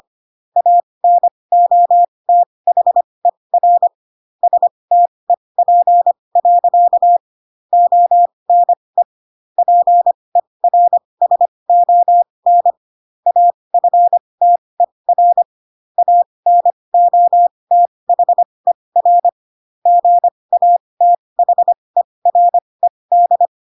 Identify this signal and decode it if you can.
Morse code, heard as a keyed tone, about 25 wpm, spaced as for 20 wpm, 700 Hz. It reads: ANOTHER STEP. ONE PERSON AFTER ANOTHER GATHERED